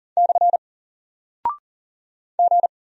Morse code, heard as a keyed tone, about 40 wpm, 700 Hz. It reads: / G
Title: Single Character (Letter:Number:Punctuation:Prosign) 40wpm